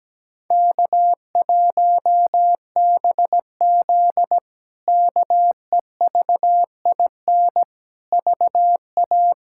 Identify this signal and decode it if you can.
Morse code, heard as a keyed tone, about 17 wpm, 700 Hz. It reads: K1BZ KEVIN VA
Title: K1USN SST QSO Practice - 17wpm